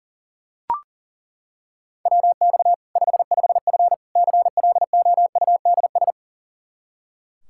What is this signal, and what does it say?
Morse code, heard as a keyed tone, about 37 wpm, 700 Hz. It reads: WX 55F CLOUDS
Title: Practice QSO - Exchange 2 - Info - 40wpm